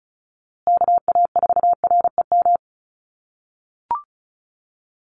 Morse code, heard as a keyed tone, about 34 wpm, 700 Hz. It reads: KA4REM
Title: USA Call Signs Repeated 2x 35wpm